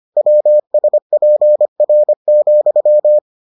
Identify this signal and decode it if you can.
Morse code, heard as a keyed tone, about 25 wpm, 600 Hz. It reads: WSPR,